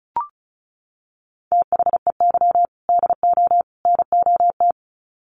Morse code, heard as a keyed tone, about 34 wpm, 700 Hz. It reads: THEY DO NOT
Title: Sets of 3 Words 35wpm